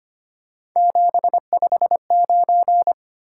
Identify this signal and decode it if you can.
Morse code, heard as a keyed tone, about 25 wpm, 700 Hz. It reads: 759
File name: Three Letter-Number Combinations - 25wpm